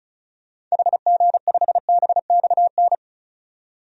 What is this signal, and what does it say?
Morse code, heard as a keyed tone, about 36 wpm, 700 Hz. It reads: HG5BXN
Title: DXCC-100 Call Signs - 6 Characters - 35wpm